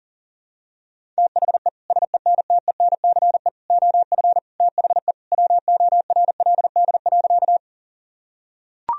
Morse code, heard as a keyed tone, about 37 wpm, 700 Hz. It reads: THE SENTENCE OF THE WORLD.
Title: Sentences from Top 1000 Words - Review Part 1 - 40wpm